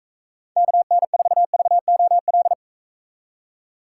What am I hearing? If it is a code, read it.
Morse code, heard as a keyed tone, about 42 wpm, 700 Hz. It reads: KN4VOL